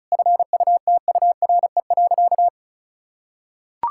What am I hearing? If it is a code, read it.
Morse code, heard as a keyed tone, about 36 wpm, 700 Hz. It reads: FUTURE.
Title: Sentences from Top 1000 Words - Review Part 6 - 35wpm